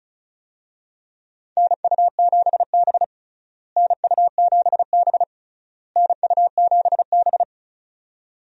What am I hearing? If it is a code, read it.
Morse code, heard as a keyed tone, about 36 wpm, 700 Hz. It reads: NU7B NU7B NU7B
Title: USA Call Signs - 4 Characters - Repeated 3x - 35wpm